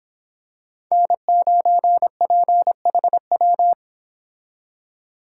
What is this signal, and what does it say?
Morse code, heard as a keyed tone, about 26 wpm, 700 Hz. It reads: N9PHW